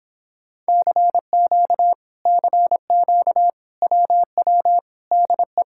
Morse code, heard as a keyed tone, about 26 wpm, 700 Hz. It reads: CQ CQ WW DE